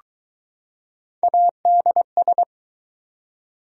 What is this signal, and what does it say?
Morse code, heard as a keyed tone, about 20 wpm, 700 Hz. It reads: ADS